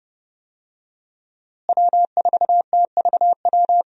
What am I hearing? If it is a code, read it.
Morse code, heard as a keyed tone, about 30 wpm, 700 Hz. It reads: W4TVW
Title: USA Call Signs - 5 Characters - Repeated 3x - 30wpm